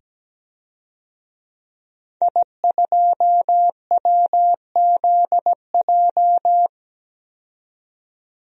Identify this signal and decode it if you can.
Morse code, heard as a keyed tone, about 17 wpm, 700 Hz. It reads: I2WZJ
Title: DXCC-100 Call Signs - 5 Characters - 17wpm